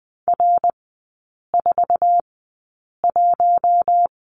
Morse code, heard as a keyed tone, about 20 wpm, spaced as for 8 wpm, 700 Hz. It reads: R41
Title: 067 Lesson 12e - Letter d - Three Characters 20wpm